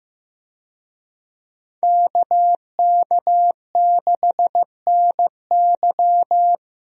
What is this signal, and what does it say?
Morse code, heard as a keyed tone, about 15 wpm, 700 Hz. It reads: KK6NY
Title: USA Call Signs - 5 Characters - Repeated 2x - 15wpm